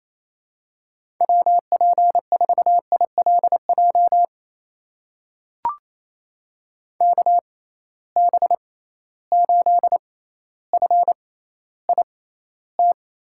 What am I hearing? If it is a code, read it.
Morse code, heard as a keyed tone, about 28 wpm, spaced as for 14 wpm, 700 Hz. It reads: WP4ILJ KB8FIT
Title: USA Call Signs - 6 Characters - 6x Character Spacing - 28wpm